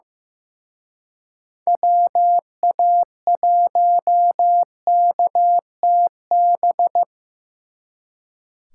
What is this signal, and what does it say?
Morse code, heard as a keyed tone, about 18 wpm, 700 Hz. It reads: WA1KTB
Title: Speed Racing - USA 2x3 Call Signs 15wpm